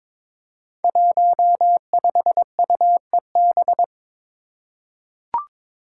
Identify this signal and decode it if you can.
Morse code, heard as a keyed tone, about 22 wpm, 700 Hz. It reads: 15UEB
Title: Five Challenging Letter-Number Combinations 22wpm